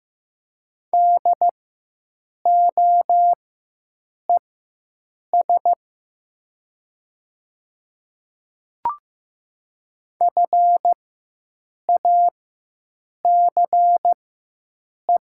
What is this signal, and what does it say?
Morse code, heard as a keyed tone, about 15 wpm, spaced as for 7 wpm, 700 Hz. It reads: DOES FACE
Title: 4x Character Spacing - Top 300 Words - 15wpm